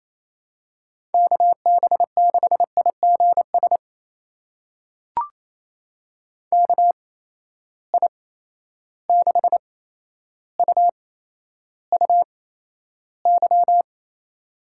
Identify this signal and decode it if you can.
Morse code, heard as a keyed tone, about 28 wpm, spaced as for 12 wpm, 700 Hz. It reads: KB6IGS KI6UUY
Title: USA Call Signs - 6 Characters - 8x Character Spacing - 28wpm